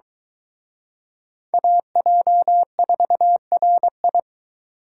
Morse code, heard as a keyed tone, about 23 wpm, 700 Hz. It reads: AJ4RI